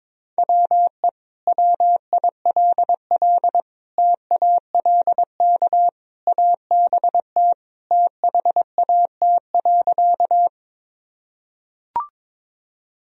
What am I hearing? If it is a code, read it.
Morse code, heard as a keyed tone, about 22 wpm, 700 Hz. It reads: WE WILL TALK ABT THAT.
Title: Sentences from Top 400 Words and Abbreviations 22wpm